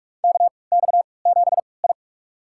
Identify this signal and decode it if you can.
Morse code, heard as a keyed tone, about 46 wpm, spaced as for 26 wpm, 700 Hz. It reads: KX7I
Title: USA Call Signs - 4 Characters - 3x Character Spacing - 45wpm